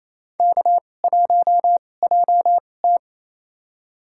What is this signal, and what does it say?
Morse code, heard as a keyed tone, about 28 wpm, spaced as for 20 wpm, 700 Hz. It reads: K1JT
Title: USA Call Signs - 2x Character Spacing - 28wpm